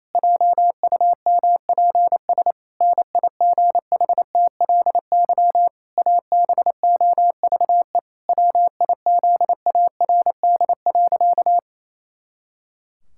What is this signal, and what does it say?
Morse code, heard as a keyed tone, about 28 wpm, 700 Hz. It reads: JUMPS NIGHTLY ABOVE WIZARD.
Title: Pangram Challenge v2 - 28wpm